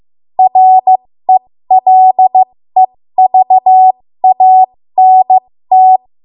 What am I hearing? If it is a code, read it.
Morse code, heard as a keyed tone, about 15 wpm, spaced as for 13 wpm, 750 Hz. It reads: RELEVANT
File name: Morse / 13WPM